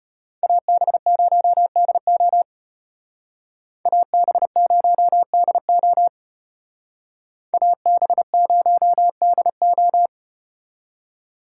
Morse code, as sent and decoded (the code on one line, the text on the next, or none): .- -... ----- -.. --- / .- -... ----- -.. --- / .- -... ----- -.. ---
AB0DO AB0DO AB0DO